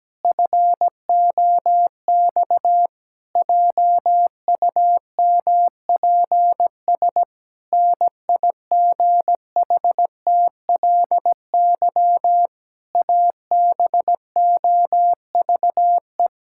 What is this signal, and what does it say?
Morse code, heard as a keyed tone, about 17 wpm, 700 Hz. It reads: FOX JUMPS NIGHTLY ABOVE
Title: Pangram Challenge v2 - 17wpm